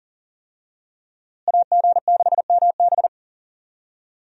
Morse code, heard as a keyed tone, about 40 wpm, 700 Hz. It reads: AG6MB
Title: USA Call Signs - 5 Characters - Repeated 2x - 40wpm